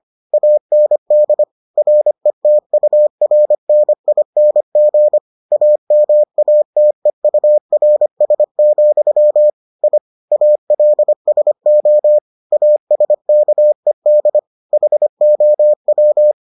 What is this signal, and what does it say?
Morse code, heard as a keyed tone, about 25 wpm, 600 Hz. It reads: AND RETURNING AMATEURS, I ALSO ASKED HOW